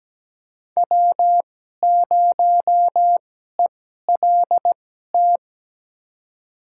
Morse code, heard as a keyed tone, about 17 wpm, spaced as for 12 wpm, 700 Hz. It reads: W0ELT
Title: USA Call Signs - 5 Characters - 2x Character Spacing - 17wpm